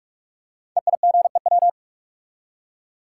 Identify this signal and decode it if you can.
Morse code, heard as a keyed tone, about 44 wpm, 700 Hz. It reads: EIGEW
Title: Code Groups - 5 Letters - 45wpm